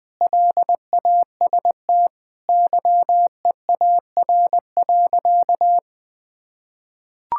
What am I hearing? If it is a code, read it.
Morse code, heard as a keyed tone, about 20 wpm, 700 Hz. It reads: LAST YEAR.
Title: Sentences with 7-Word Phrases - 20wpm